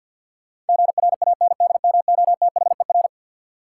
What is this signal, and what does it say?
Morse code, heard as a keyed tone, about 52 wpm, 700 Hz. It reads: GRANDMOTHER
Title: Sentences from Top 1200 Words - New Word Drill Alt - 50wpm